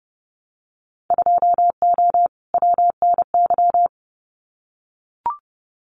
Morse code, heard as a keyed tone, about 30 wpm, 700 Hz. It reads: 2O WNY
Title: Winter Field Day Exchanges Encore 30wpm